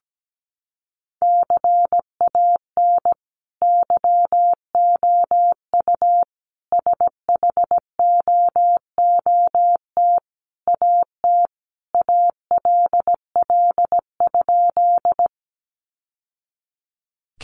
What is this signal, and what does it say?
Morse code, heard as a keyed tone, about 17 wpm, 700 Hz. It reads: CAN YOU SHOOT AT ALL?
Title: Sentences from Top 1200 Words 17wpm